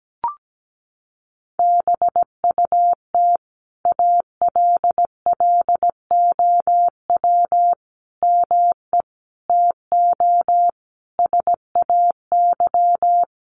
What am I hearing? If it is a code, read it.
Morse code, heard as a keyed tone, about 17 wpm, 700 Hz. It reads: BUT ALLOW ME TO SAY
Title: Sentences from Top 700 Words 17wpm